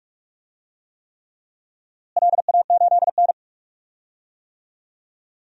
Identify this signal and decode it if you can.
Morse code, heard as a keyed tone, about 43 wpm, 700 Hz. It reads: LA8N
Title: DXCC-100 Call Signs - 4 Characters - Repeated 3x - 45wpm